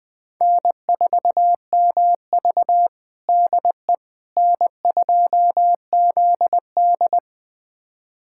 Morse code, heard as a keyed tone, about 21 wpm, 700 Hz. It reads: N4MV DE N2ZD
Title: USA Callsign DE Callsign - 4 Characters - 20wpm